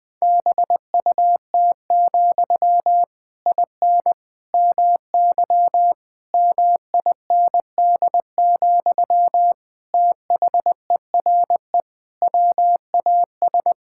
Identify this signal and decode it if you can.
Morse code, heard as a keyed tone, about 20 wpm, 700 Hz. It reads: BUT, IN MY MIND, THERE WAS
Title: Sentences from Top 1000 Words - Review Part 5 - 20wpm